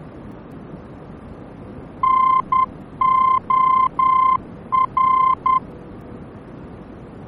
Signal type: Morse code